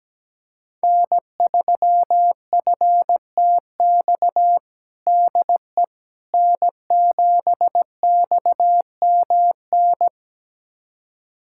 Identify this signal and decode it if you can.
Morse code, heard as a keyed tone, about 17 wpm, 700 Hz. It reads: N3FTX DE N7XMN
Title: USA Callsign DE Callsign - 5 Characters - 17wpm